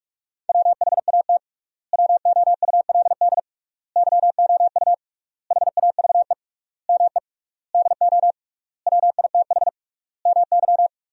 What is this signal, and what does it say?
Morse code, heard as a keyed tone, about 44 wpm, 700 Hz. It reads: WHAT WOULD YOU HAVE ME DO WITH MY
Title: 3x Word-Spacing - Sentences from Top 300 Words - 45wpm